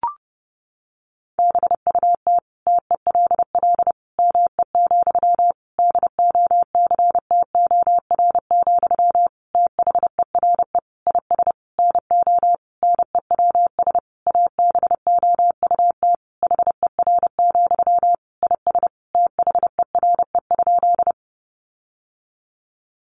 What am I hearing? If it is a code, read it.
Morse code, heard as a keyed tone, about 30 wpm, 700 Hz. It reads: BUT TELL ME, DOCTOR, THERE IS NO NEWS ABOUT HER, IS THERE?
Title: Sentences from Top 1500 Words 30wpm